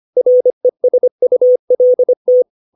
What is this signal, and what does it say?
Morse code, heard as a keyed tone, about 25 wpm, 500 Hz. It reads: RESULT
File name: 632 words / 632 Words 25WPM